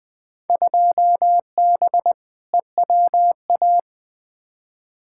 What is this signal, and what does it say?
Morse code, heard as a keyed tone, about 20 wpm, 700 Hz. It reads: 2B EWA
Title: Field Day Exchanges Encore - 20wpm